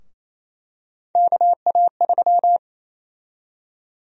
Morse code, heard as a keyed tone, about 27 wpm, 700 Hz. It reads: KA3
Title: USA Call Sign Prefixes - Repeated 2x - 28wpm